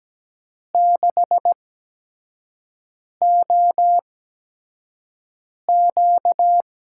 Morse code, heard as a keyed tone, about 17 wpm, spaced as for 5 wpm, 700 Hz. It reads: 6OQ
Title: USA Call Signs - 6 Characters - 8x Character Spacing - 17wpm